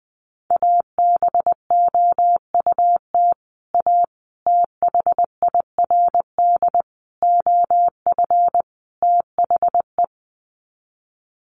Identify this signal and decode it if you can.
Morse code, heard as a keyed tone, about 20 wpm, 700 Hz. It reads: ABOUT A THIRD OF THE
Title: Sets of 5 Words Encore2 20wpm